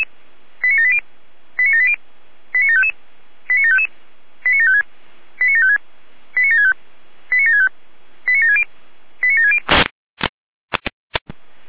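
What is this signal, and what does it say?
Five-tone selective call (ZVEI), heard as quick runs of five tones, tones of about 70 ms. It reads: 89780 89780 89755 89755 89756 89756 89765 89765 89760 89760
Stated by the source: Five-tone ZVEI selective call 89126, received by SDR.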